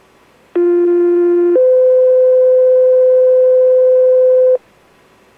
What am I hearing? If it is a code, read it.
Two-tone page, heard as a one-second tone, then a three-second tone, about 350 then 510 Hz. It is heard from 0.6 to 4.6 s.